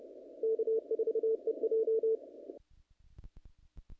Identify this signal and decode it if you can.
Morse code, heard as a keyed tone, about 30 wpm, 450 Hz. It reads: K4EJ
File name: contests / naqp-cw-k5tr-op-k5pi- -1900